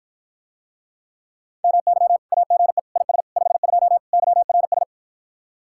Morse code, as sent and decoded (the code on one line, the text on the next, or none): -- -.-- / .- --. / .. ... / ..... ..--- / -.-- .-. ...
MY AGE IS 52 YRS